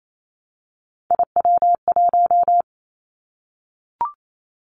Morse code, heard as a keyed tone, about 28 wpm, 700 Hz. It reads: IW1
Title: Callsign Prefixes for Top 25 DXCC - 28wpm